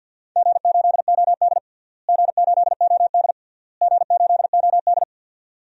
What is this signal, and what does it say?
Morse code, heard as a keyed tone, about 50 wpm, 700 Hz. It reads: G8OD G8OD G8OD
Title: DXCC-100 Call Signs - 4 Characters - Repeated 3x - 50wpm